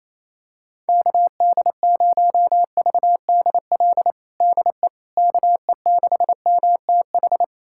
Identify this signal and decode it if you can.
Morse code, heard as a keyed tone, about 28 wpm, 700 Hz. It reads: KD0VDL DE KE6MTH
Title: USA Callsign DE Callsign - 6 Characters - 28wpm